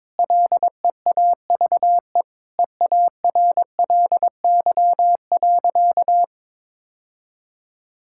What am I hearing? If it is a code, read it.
Morse code, heard as a keyed tone, about 22 wpm, 700 Hz. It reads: LEAVE EARLY.
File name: practice / 22wpm / Sentences with 3-Word Phrases - 22wpm